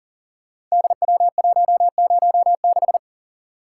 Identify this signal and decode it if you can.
Morse code, heard as a keyed tone, about 39 wpm, 690 Hz. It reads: DW106
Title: Code Groups - 5 Letter:Numbers - 40wpm